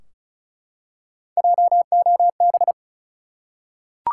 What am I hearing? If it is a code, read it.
Morse code, heard as a keyed tone, about 34 wpm, 700 Hz. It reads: JOB
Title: Three Letter Words - 35wpm